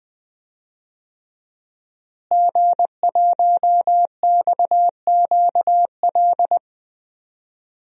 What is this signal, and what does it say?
Morse code, heard as a keyed tone, about 20 wpm, 700 Hz. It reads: G1XQL
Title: DXCC-100 Call Signs - 5 Characters - Repeated 3x - 20wpm